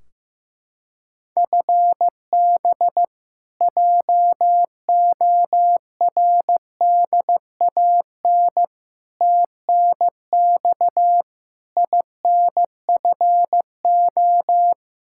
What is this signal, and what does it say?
Morse code, heard as a keyed tone, about 16 wpm, 700 Hz. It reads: FB JORDAN TNX INFO
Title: Practice QSO - Exchange 3 - More Info - 15wpm